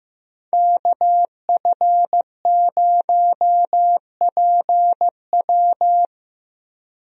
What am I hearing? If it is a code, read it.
Morse code, heard as a keyed tone, about 15 wpm, 700 Hz. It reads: KF0PW
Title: USA Call Signs - 5 Characters - Repeated 3x - 15wpm